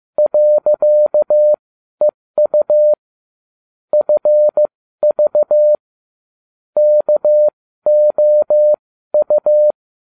Morse code, heard as a keyed tone, about 15 wpm, spaced as for 12 wpm, 600 Hz. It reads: .EU FV KOU